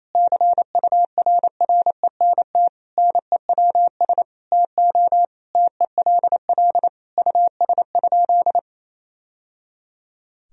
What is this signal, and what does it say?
Morse code, heard as a keyed tone, about 28 wpm, 700 Hz. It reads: CURRENT NEWS TO TELL US?
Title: Sentences from Top 1000 Words - Review Part 4 - 28wpm